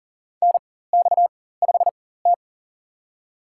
Morse code, heard as a keyed tone, about 40 wpm, spaced as for 19 wpm, 700 Hz. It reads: NX5T
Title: USA Call Signs - 4x Character Spacing - 40wpm